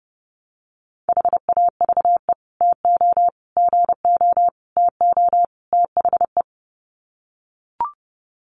Morse code, heard as a keyed tone, about 30 wpm, 700 Hz. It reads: HAVE TO GO TO THE
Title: Sets of 5 Words Encore 30wpm